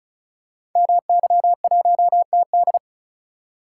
Morse code, heard as a keyed tone, about 36 wpm, 690 Hz. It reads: MY1TD
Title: Code Groups - 5 Letter:Numbers - 35wpm